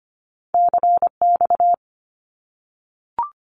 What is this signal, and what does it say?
Morse code, heard as a keyed tone, about 25 wpm, 700 Hz. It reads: CX
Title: Abbreviations 25wpm